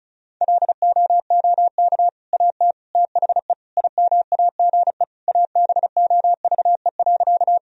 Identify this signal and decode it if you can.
Morse code, heard as a keyed tone, about 36 wpm, 700 Hz. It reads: LOOK AT THE IMAGE ABOVE.
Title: Sentences from Top 1000 Words - Review Part 4 - 35wpm